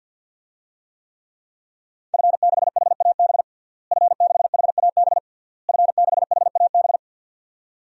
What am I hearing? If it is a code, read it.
Morse code, heard as a keyed tone, about 51 wpm, 700 Hz. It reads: F6HAB F6HAB F6HAB
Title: DXCC-100 Call Signs - 5 Characters - Repeated 3x - 50wpm